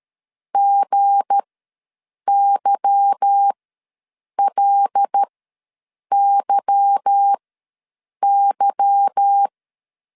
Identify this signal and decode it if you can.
Morse code, heard as a keyed tone, about 13 wpm, spaced as for 7 wpm, 800 Hz. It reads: GYLYY
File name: LEZIONI / 8 YJ45 / 8 Y ws 12wpm 3409